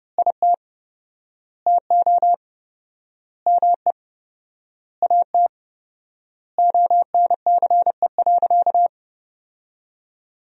Morse code, heard as a keyed tone, about 30 wpm, 700 Hz. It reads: IT TO ME AT ONCE.
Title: 4x Word-Spacing - Sentences from Top 500 Words - 30wpm